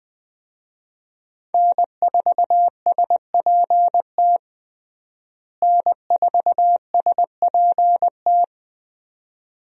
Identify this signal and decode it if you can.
Morse code, heard as a keyed tone, about 20 wpm, 700 Hz. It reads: N4SPT N4SPT
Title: USA Call Signs - 5 Characters - Repeated 2x - 20wpm